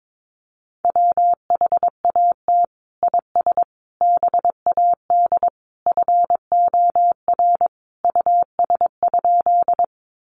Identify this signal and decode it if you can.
Morse code, heard as a keyed tone, about 22 wpm, 700 Hz. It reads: WHAT IS BAD FOR US?
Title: Sentences from Top 700 Words 22wpm